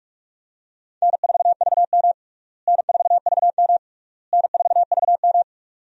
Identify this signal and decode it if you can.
Morse code, heard as a keyed tone, about 46 wpm, 700 Hz. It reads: N4VM N4VM N4VM
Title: USA Call Signs - 4 Characters - Repeated 3x - 45wpm